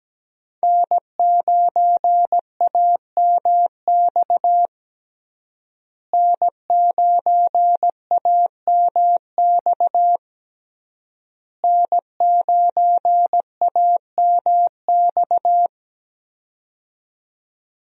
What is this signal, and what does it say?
Morse code, heard as a keyed tone, about 17 wpm, 700 Hz. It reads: N9AMX N9AMX N9AMX
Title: USA Call Signs - 5 Characters - Repeated 3x - 17wpm